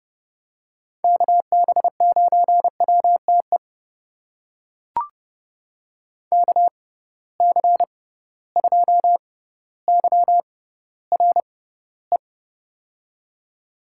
Morse code, heard as a keyed tone, about 30 wpm, spaced as for 15 wpm, 700 Hz. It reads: KB9WTE KC2YRE
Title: USA Call Signs - 6 Characters - 6x Character Spacing - 30wpm